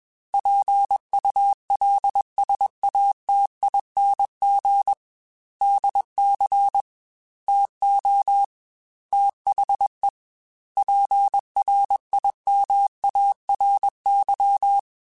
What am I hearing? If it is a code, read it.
Morse code, heard as a keyed tone, about 21 wpm, 800 Hz. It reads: PULSATING DC TO THE PRIMARY
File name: Technical Subjects / Technical Subjects-mp3s / Sw Pwr Sup @20